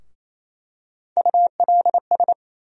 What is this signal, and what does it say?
Morse code, heard as a keyed tone, about 28 wpm, 700 Hz. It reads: ULS
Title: USA Call Sign Postfixes - 3 Letters 28wpm